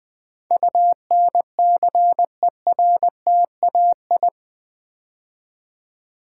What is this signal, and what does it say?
Morse code, heard as a keyed tone, about 20 wpm, 700 Hz. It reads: UNCERTAI
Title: Build A Word - 4 Letters At A Time - 11-Letter Words - 20wpm